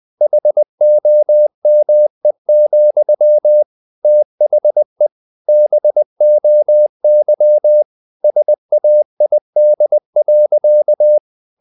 Morse code, heard as a keyed tone, about 20 wpm, 600 Hz. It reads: HOME, THE BOY SAID.
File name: the old man and the sea / the old man and the sea part 1 20wpm